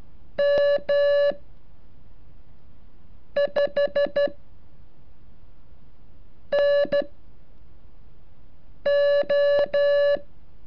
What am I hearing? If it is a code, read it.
Morse code, heard as a keyed tone, about 13 wpm, spaced as for 4 wpm, 570 Hz. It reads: M5NO
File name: curso / ejercicio 37